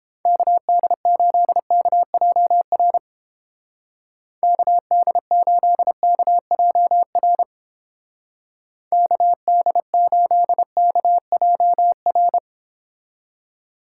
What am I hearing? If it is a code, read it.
Morse code, heard as a keyed tone, about 29 wpm, 700 Hz. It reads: KD8KJR KD8KJR KD8KJR